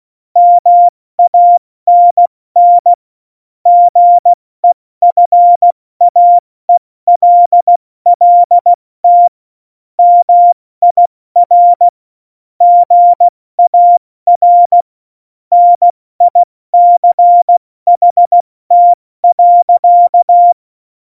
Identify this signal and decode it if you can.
Morse code, heard as a keyed tone, about 16 wpm, 700 Hz. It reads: MANN GEFAELLT MIR GAR NICHT.
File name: Morsekurs / Witze / Witz 04. Februar 2020, 70 F80